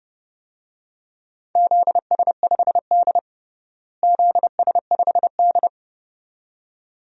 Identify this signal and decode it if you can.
Morse code, heard as a keyed tone, about 30 wpm, 700 Hz. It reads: ZS5D ZS5D
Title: DXCC-100 Call Signs - 4 Characters - Repeated 2x - 30wpm